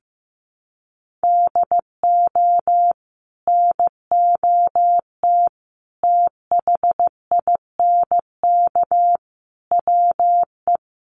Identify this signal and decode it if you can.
Morse code, heard as a keyed tone, about 15 wpm, 700 Hz. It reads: DO NOT THINK WE
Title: Sets of 5 Words Encore2 15wpm